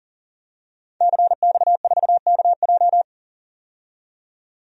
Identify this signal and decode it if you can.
Morse code, heard as a keyed tone, about 38 wpm, 700 Hz. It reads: CX4KJ
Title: DXCC-100 Call Signs - 5 Characters - 40wpm